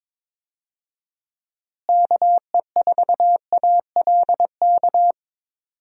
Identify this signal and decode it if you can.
Morse code, heard as a keyed tone, about 22 wpm, 700 Hz. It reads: KE4ALK
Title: USA Call Signs - 6 Characters - Repeated 3x - 22wpm